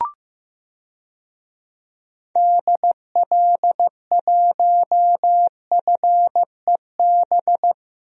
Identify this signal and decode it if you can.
Morse code, heard as a keyed tone, about 15 wpm, 700 Hz. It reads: DL1FEB
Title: DXCC-100 Call Signs - 6 Characters - Repeated 3x - 15wpm